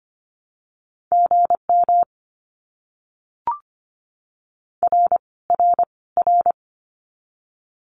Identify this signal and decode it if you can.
Morse code, heard as a keyed tone, about 24 wpm, 700 Hz. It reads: GM R R R
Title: Abbreviations Repeated 3x - 25wpm